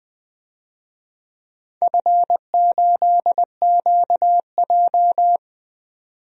Morse code, heard as a keyed tone, about 20 wpm, 700 Hz. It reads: F8QJ
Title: DXCC-100 Call Signs - 4 Characters - 20wpm